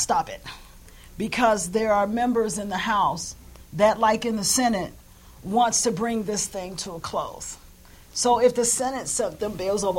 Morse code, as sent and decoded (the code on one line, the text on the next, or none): none